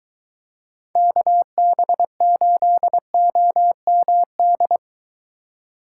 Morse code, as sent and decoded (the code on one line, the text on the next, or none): -.- -... ---.. --- -- -..
KB8OMD